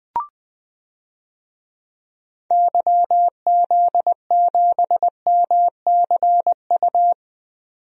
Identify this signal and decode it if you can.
Morse code, heard as a keyed tone, about 20 wpm, 700 Hz. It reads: YZ7MCU
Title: DXCC-100 Call Signs - 6 Characters - 20wpm